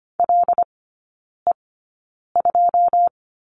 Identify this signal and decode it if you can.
Morse code, heard as a keyed tone, about 25 wpm, spaced as for 9 wpm, 700 Hz. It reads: LE2
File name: course / 25:10wpm / 093 Lesson 16e - Letter c - Three Characters 25wpm